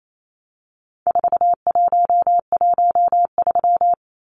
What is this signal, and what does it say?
Morse code, heard as a keyed tone, about 28 wpm, 700 Hz. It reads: 4113
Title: Numbers - 4-Digit 28wpm